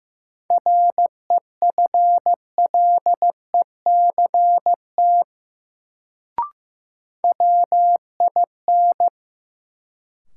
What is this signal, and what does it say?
Morse code, heard as a keyed tone, about 15 wpm, 700 Hz. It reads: REFLECT WIN
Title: Sentences from Top 1500 Words - New Word Drill - 15wpm